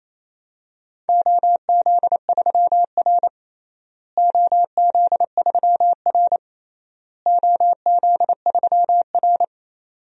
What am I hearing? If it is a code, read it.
Morse code, heard as a keyed tone, about 28 wpm, 700 Hz. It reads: OZ3R OZ3R OZ3R
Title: DXCC-100 Call Signs - 4 Characters - Repeated 3x - 28wpm